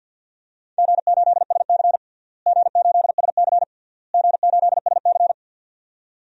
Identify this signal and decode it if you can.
Morse code, heard as a keyed tone, about 52 wpm, 700 Hz. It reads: G8SC G8SC G8SC
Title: DXCC-100 Call Signs - 4 Characters - Repeated 3x - 50wpm